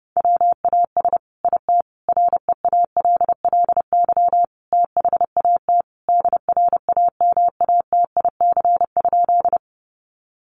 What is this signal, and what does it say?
Morse code, heard as a keyed tone, about 30 wpm, 700 Hz. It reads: WAS IT REALLY THAT DRAMATIC?
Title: Sentences from Top 1300 Words 30wpm